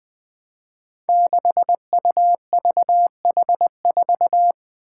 Morse code, heard as a keyed tone, about 20 wpm, 700 Hz. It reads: 6UVH4
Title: Five Challenging Letter-Number Combinations 20wpm